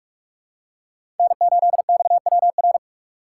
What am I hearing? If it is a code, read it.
Morse code, heard as a keyed tone, about 42 wpm, 700 Hz. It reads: N8XWR
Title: USA Call Signs - 2x Character Spacing - 45wpm